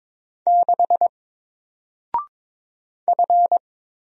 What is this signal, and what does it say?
Morse code, heard as a keyed tone, about 22 wpm, 700 Hz. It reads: 6 F
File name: Single-Letter-Number-22wpm